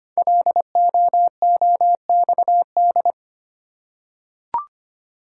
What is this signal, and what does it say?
Morse code, heard as a keyed tone, about 25 wpm, 700 Hz. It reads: LOOXD
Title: Code Groups - 5 Letter:Numbers - 25wpm